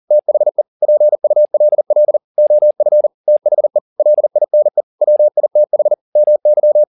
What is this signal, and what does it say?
Morse code, heard as a keyed tone, about 40 wpm, 600 Hz. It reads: THE PULL OF THE LINE WITH MY
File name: the old man and the sea 2 40wpm